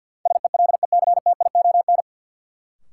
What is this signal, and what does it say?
Morse code, heard as a keyed tone, about 52 wpm, 700 Hz. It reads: SELECTION
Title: Sentences from Top 1400 Words - New Word Drill - 50wpm